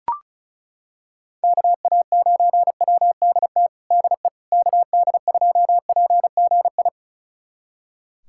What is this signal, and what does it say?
Morse code, heard as a keyed tone, about 36 wpm, 700 Hz. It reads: KA9WDT DE KD2PGI
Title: USA Callsign DE Callsign - 6 Characters - 35wpm